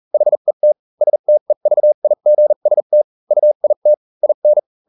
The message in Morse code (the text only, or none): HET STEVIGST UIT IN